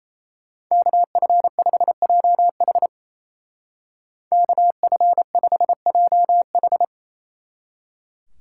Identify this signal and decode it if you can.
Morse code, heard as a keyed tone, about 34 wpm, 700 Hz. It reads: KF5JH KF5JH
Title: Speed Racing - USA 2x2 Call Signs 28wpm